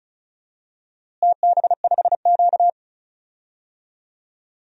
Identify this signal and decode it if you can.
Morse code, heard as a keyed tone, about 34 wpm, 700 Hz. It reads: TB5Q
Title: DXCC-100 Call Signs - 4 Characters - Repeated 3x - 35wpm